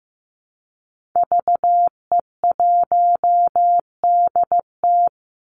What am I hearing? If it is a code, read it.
Morse code, heard as a keyed tone, about 15 wpm, 700 Hz. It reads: VE1DT
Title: Call Signs Repeated 3x - 15wpm